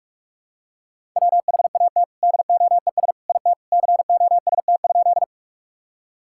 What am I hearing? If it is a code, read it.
Morse code, heard as a keyed tone, about 41 wpm, 700 Hz. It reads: WHAT DOES IT COST?
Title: Sentences from Top 1000 Words - Review Part 5 - 45wpm